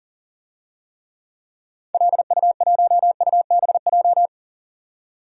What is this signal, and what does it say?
Morse code, heard as a keyed tone, about 39 wpm, 700 Hz. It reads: LU1UBJ
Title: DXCC-100 Call Signs - 6 Characters - 40wpm